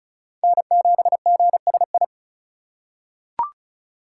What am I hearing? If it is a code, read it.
Morse code, heard as a keyed tone, about 36 wpm, 700 Hz. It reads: N7GSI
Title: USA Call Signs - 5 Characters - 3x Character Spacing - 35wpm